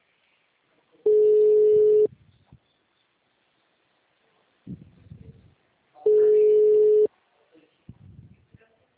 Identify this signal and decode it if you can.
Ringback tone (Europe, 425 Hz), with speech.